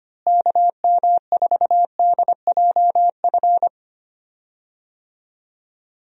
Morse code, heard as a keyed tone, about 25 wpm, 700 Hz. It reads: KM4DJF
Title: USA Call Signs - 6 Characters - Repeated 2x - 25wpm